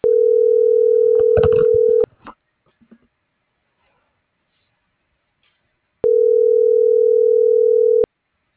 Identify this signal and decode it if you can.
Ringback tone (North America), 440 + 480 Hz.